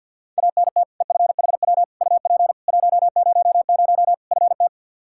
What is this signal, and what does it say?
Morse code, heard as a keyed tone, about 52 wpm, 700 Hz. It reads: ANT EFHW UP 100 FT
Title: Practice QSO - Exchange 2 - Info - 50wpm